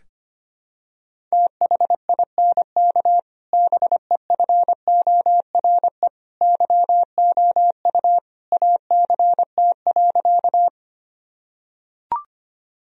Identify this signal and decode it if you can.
Morse code, heard as a keyed tone, about 25 wpm, 700 Hz. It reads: THINK BEFORE YOU ACT.
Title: Sentences from Top 1000 Words - Review Part 1 - 25wpm